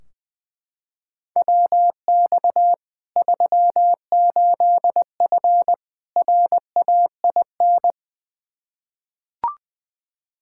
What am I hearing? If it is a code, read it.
Morse code, heard as a keyed tone, about 20 wpm, 700 Hz. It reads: WX 38F RAIN
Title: Practice QSO - Exchange 2 - Info - 20wpm